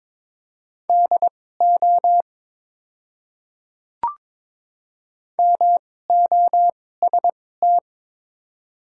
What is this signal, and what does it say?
Morse code, heard as a keyed tone, about 22 wpm, spaced as for 16 wpm, 700 Hz. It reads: DO MOST
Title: 2x Character Spacing - Top 200 Words - 22wpm